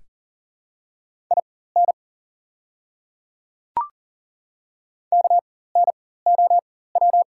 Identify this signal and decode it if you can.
Morse code, heard as a keyed tone, about 40 wpm, spaced as for 19 wpm, 690 Hz. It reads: IN KNOW
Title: 4x Character Spacing - Top 100 Words - 40wpm